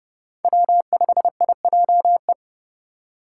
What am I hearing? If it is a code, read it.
Morse code, heard as a keyed tone, about 37 wpm, 700 Hz. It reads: W5IJE
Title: Speed Racing - USA 1x3 Call Signs 30wpm